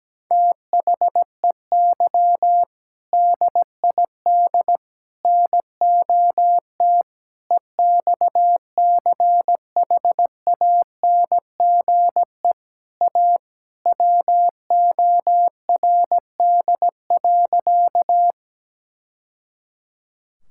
Morse code, heard as a keyed tone, about 17 wpm, 700 Hz. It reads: THEY DID NOT EXCHANGE A WORD.
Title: Sentences from Top 1000 Words - Review Part 6 - 17wpm